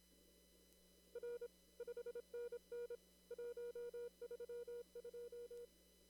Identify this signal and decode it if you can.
Morse code, heard as a keyed tone, about 26 wpm, 480 Hz. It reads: R 5NN 132
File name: CW / kt5j wpx cw-op-k5pi